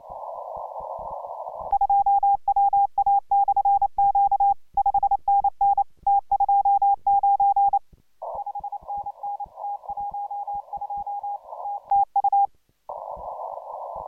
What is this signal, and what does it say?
Morse code, heard as a keyed tone, about 28 wpm, 800 Hz. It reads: 2WA/Q 5NN T29 5NN T12 TU